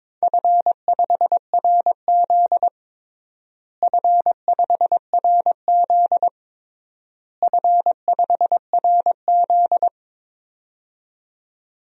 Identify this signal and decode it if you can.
Morse code, heard as a keyed tone, about 22 wpm, 700 Hz. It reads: F5RZ F5RZ F5RZ
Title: DXCC-100 Call Signs - 4 Characters - Repeated 3x - 22wpm